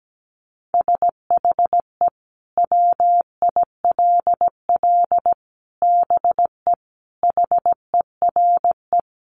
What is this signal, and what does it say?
Morse code, heard as a keyed tone, about 17 wpm, 700 Hz. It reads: SHE WILL BE HERE
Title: Sentences from Top 600 Words 17wpm